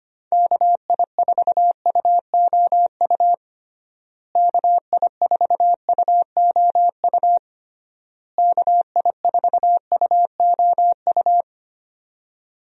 Morse code, heard as a keyed tone, about 25 wpm, 700 Hz. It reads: KI4UOU KI4UOU KI4UOU
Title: USA Call Signs - 6 Characters - Repeated 3x - 25wpm